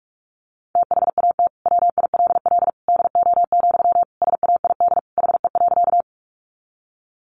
Morse code, heard as a keyed tone, about 44 wpm, 700 Hz. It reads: THAT WILL DO, SAID HE.
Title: Sentences from Top 100 Words 45wpm